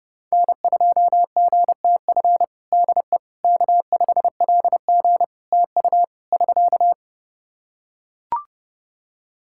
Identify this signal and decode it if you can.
Morse code, heard as a keyed tone, about 30 wpm, 700 Hz. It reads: N2GTF DE K5LG TU <SK>
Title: Practice QSO - Exchange 4 - Ending - 30wpm